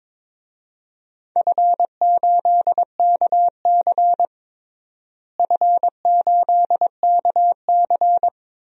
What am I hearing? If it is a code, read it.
Morse code, heard as a keyed tone, about 22 wpm, 700 Hz. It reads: F8KC F8KC
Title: DXCC-100 Call Signs - 4 Characters - Repeated 3x - 22wpm